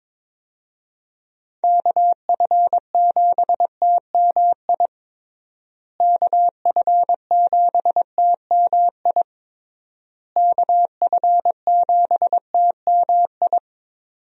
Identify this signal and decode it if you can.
Morse code, heard as a keyed tone, about 22 wpm, 700 Hz. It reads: KF7TMI KF7TMI KF7TMI
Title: USA Call Signs - 6 Characters - Repeated 3x - 22wpm